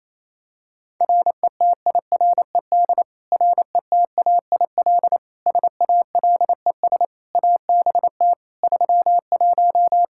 Morse code, heard as a keyed tone, about 27 wpm, 700 Hz. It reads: RETIRED RETAIL SALES ABT 31
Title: Practice QSO - Exchange 3 - More Info - 28wpm